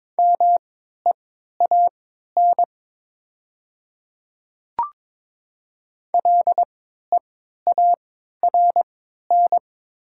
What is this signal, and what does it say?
Morse code, heard as a keyed tone, about 22 wpm, spaced as for 12 wpm, 700 Hz. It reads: MEAN LEARN
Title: 3x Character Spacing - Top 300 Words - 22wpm